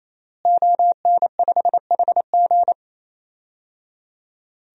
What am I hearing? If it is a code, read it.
Morse code, heard as a keyed tone, about 28 wpm, 700 Hz. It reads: ON5HG
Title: DXCC-100 Call Signs - 5 Characters - Repeated 3x - 28wpm